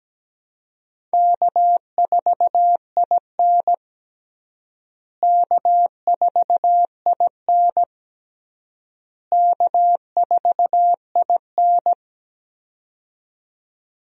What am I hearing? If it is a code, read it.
Morse code, heard as a keyed tone, about 17 wpm, 700 Hz. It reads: K4IN K4IN K4IN
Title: USA Call Signs - 4 Characters - Repeated 3x - 17wpm